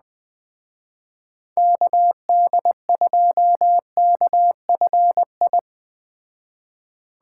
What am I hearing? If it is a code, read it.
Morse code, heard as a keyed tone, about 21 wpm, 700 Hz. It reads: KD2KFI
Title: Speed Racing - USA Call Signs 17wpm